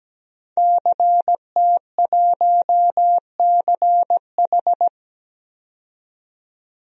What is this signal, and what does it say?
Morse code, heard as a keyed tone, about 17 wpm, 700 Hz. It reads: CT1CH
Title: DXCC-100 Call Signs - 5 Characters - 17wpm